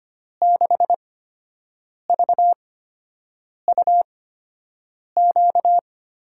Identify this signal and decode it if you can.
Morse code, heard as a keyed tone, about 25 wpm, spaced as for 7 wpm, 700 Hz. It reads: 6VUQ
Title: USA Call Signs - 8x Character Spacing - 25wpm